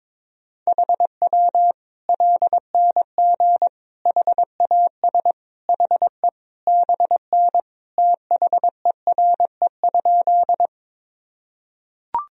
Morse code, heard as a keyed tone, about 22 wpm, 700 Hz. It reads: HW LNG HAS HE BN THERE?
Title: Sentences from Top 100 Words and Abbreviations 22wpm